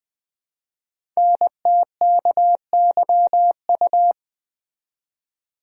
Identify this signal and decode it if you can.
Morse code, heard as a keyed tone, about 20 wpm, 700 Hz. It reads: NTKYU
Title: Code Groups - 5 Letters - 20wpm